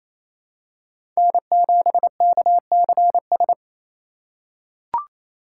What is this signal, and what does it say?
Morse code, heard as a keyed tone, about 28 wpm, 700 Hz. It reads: N7KCS
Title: USA Call Signs - 4x Character Spacing - 28wpm